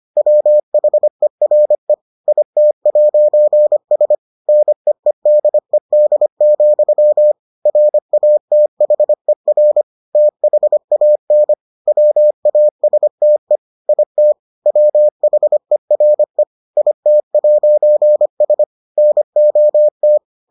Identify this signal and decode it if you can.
Morse code, heard as a keyed tone, about 25 wpm, 600 Hz. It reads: WHERE IT'S NEEDED, RATHER THAN WASTE IT WHERE IT'S NOT